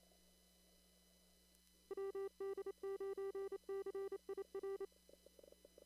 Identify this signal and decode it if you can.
Morse code, heard as a keyed tone, about 28 wpm, 390 Hz. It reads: WD9CIR
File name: CW / kt5j wpx cw-op-k5pi